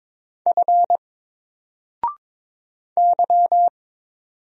Morse code, heard as a keyed tone, about 22 wpm, 700 Hz. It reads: F Y
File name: Single-Letter-Number-22wpm